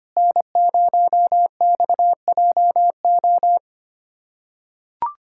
Morse code, heard as a keyed tone, about 25 wpm, 700 Hz. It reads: N0XJO
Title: USA Callsign DE Callsign - 5 Characters - 25wpm